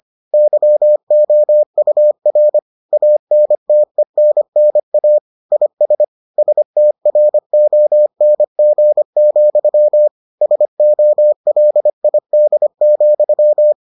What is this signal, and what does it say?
Morse code, heard as a keyed tone, about 25 wpm, 600 Hz. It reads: YOUR ANTENNA IS STRONG, SOLID,